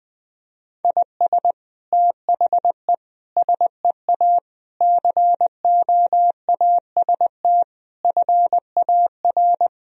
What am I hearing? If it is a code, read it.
Morse code, heard as a keyed tone, about 20 wpm, 700 Hz. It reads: IS THE SEA COAST FAR
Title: Sentences from Top 1000 Words - Review Part 4 - 20wpm